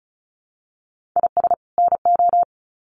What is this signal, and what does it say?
Morse code, heard as a keyed tone, about 34 wpm, 700 Hz. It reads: IS NO
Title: Sets of 2 Words 35wpm